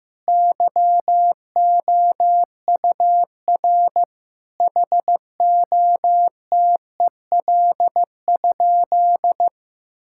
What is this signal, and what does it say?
Morse code, heard as a keyed tone, about 15 wpm, 700 Hz. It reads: YOUR HOTEL?
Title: Sentences from Top 1600 Words 15wpm